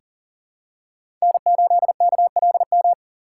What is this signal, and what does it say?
Morse code, heard as a keyed tone, about 40 wpm, 700 Hz. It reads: N8KLM
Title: USA Call Signs - 5 Characters - 4x Character Spacing - 40wpm